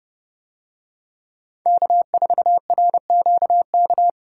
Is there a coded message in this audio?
Yes: Morse code from 1.7 to 4.1 s, heard as a keyed tone, about 29 wpm, 700 Hz. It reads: K4RQK